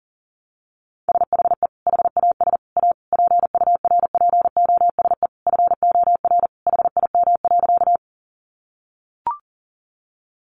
Morse code, heard as a keyed tone, about 40 wpm, 700 Hz. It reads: SHE HAS A PURPOSE FOR HIM.
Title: Sentences from Top 800 Words 40wpm